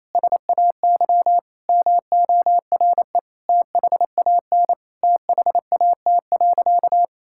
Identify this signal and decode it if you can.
Morse code, heard as a keyed tone, about 28 wpm, 700 Hz. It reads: SAY MORE THAN THAT.
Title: Sentences from Top 1600 Words 28wpm